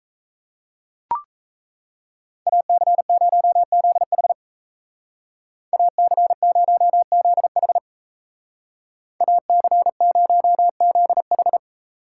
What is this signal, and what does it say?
Morse code, heard as a keyed tone, about 34 wpm, 700 Hz. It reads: AC0ZH AC0ZH AC0ZH
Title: Speed Racing - USA 2x2 Call Signs 28wpm